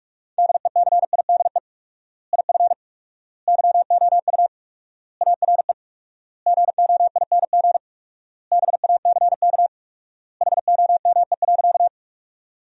Morse code, heard as a keyed tone, about 42 wpm, 700 Hz. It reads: DECIDE IF YOU ARE GOING BACK HOME.
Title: 4x Word-Spacing - Sentences from Top 500 Words - 45wpm